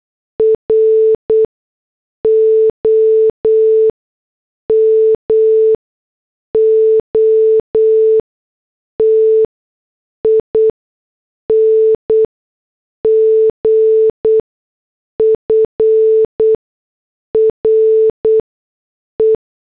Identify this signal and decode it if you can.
Morse code, heard as a keyed tone, about 8 wpm, spaced as for 6 wpm, 440 Hz. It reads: ROMOTING FRE
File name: morse / headlines